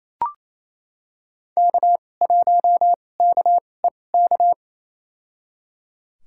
Morse code, heard as a keyed tone, about 28 wpm, spaced as for 20 wpm, 700 Hz. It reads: K1KEK
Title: USA Call Signs - 5 Characters - 2x Character Spacing - 28wpm